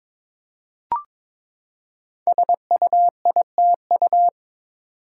Morse code, heard as a keyed tone, about 22 wpm, 700 Hz. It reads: SUITU
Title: Code Groups - 5 Letters - 22wpm